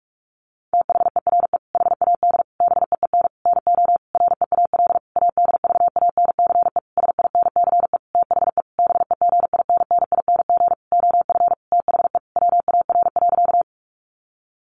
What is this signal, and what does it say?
Morse code, heard as a keyed tone, about 44 wpm, 700 Hz. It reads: THERE HAD BEEN NO REAL ADVANCE SINCE THE BEGINNING OF THE WAR.
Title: Sentences from Top 1400 Words 45wpm